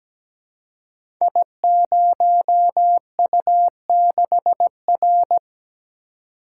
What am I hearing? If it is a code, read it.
Morse code, heard as a keyed tone, about 17 wpm, 700 Hz. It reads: I0U6R
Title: Code Groups - 5 Letter:Numbers - 17wpm